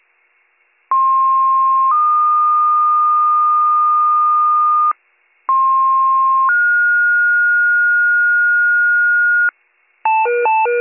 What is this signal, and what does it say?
Two-tone page, heard as a one-second tone, then a three-second tone, about 1060 then 1220 Hz, about 1060 then 1430 Hz, with noise. It is heard from 0.9 to 9.5 s.